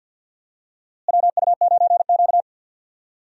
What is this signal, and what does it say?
Morse code, heard as a keyed tone, about 50 wpm, 700 Hz. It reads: WU9Q
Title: USA Call Signs - 4 Characters - Repeated 2x - 50wpm